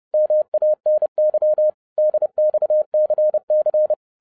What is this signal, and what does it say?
Morse code, heard as a keyed tone, about 30 wpm, 610 Hz. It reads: MANY DXCC
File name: 30wpm